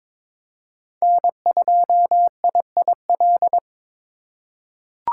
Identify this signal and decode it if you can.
Morse code, heard as a keyed tone, about 22 wpm, 700 Hz. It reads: N2IIL
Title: USA Call Signs - 3x Character Spacing - 22wpm